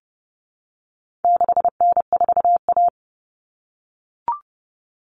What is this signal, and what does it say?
Morse code, heard as a keyed tone, about 30 wpm, 700 Hz. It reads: 6N4A
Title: Common Cut Numbers - 4-Digit 30wpm